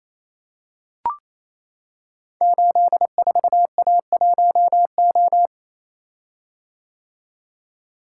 Morse code, heard as a keyed tone, about 28 wpm, 700 Hz. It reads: 84A1O
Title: Code Groups - 5 Letter:Numbers - 28wpm